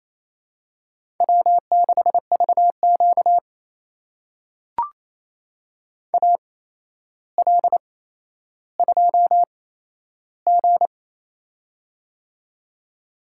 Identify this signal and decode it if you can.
Morse code, heard as a keyed tone, about 28 wpm, spaced as for 12 wpm, 700 Hz. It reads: W6VQ AL2G
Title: USA Call Signs - 4 Characters - 8x Character Spacing - 28wpm